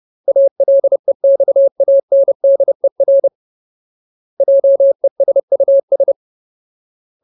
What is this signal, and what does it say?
Morse code, heard as a keyed tone, about 30 wpm, 550 Hz. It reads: ALEXANDER JESUS
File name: Ultimate Word List / 30WPM Ultimate